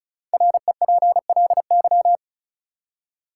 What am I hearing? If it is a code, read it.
Morse code, heard as a keyed tone, about 36 wpm, 700 Hz. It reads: REPLY
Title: Five Letter Words - 35wpm